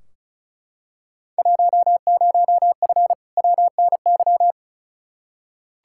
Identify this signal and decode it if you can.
Morse code, heard as a keyed tone, about 34 wpm, 700 Hz. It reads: 10F WNY
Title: Field Day Exchanges Encore - 35wpm